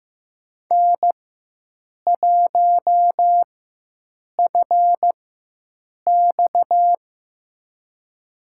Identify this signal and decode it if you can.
Morse code, heard as a keyed tone, about 16 wpm, spaced as for 7 wpm, 700 Hz. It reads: N1FX
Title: USA Call Signs - 4 Characters - 4x Character Spacing - 15wpm